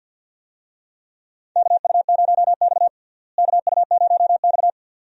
Morse code, heard as a keyed tone, about 50 wpm, 700 Hz. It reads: KU0X KU0X
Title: USA Call Signs - 4 Characters - Repeated 2x - 50wpm